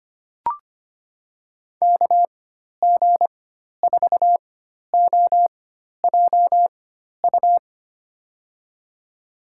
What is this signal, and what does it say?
Morse code, heard as a keyed tone, about 25 wpm, spaced as for 12 wpm, 700 Hz. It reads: KG4OJU
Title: USA Call Signs - 6 Characters - 4x Character Spacing - 25wpm